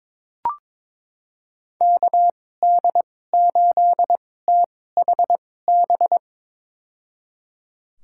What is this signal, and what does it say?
Morse code, heard as a keyed tone, about 22 wpm, spaced as for 16 wpm, 700 Hz. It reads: KD8THB
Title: USA Call Signs - 6 Characters - 2x Character Spacing - 22wpm